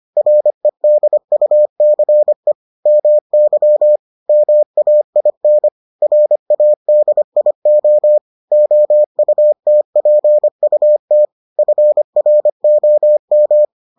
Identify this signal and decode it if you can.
Morse code, heard as a keyed tone, about 25 wpm, 600 Hz. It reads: REDUCE MY MAIN RADIO OUTPUT FROM